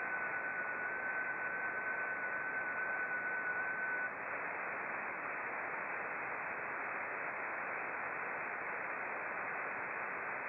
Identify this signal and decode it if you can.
FT8.
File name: contests / sept-vhf-k5tr- -0300